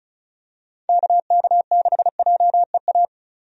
Morse code, heard as a keyed tone, about 36 wpm, 700 Hz. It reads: KK6JEA
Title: USA Call Signs - 6 Characters - Repeated 2x - 35wpm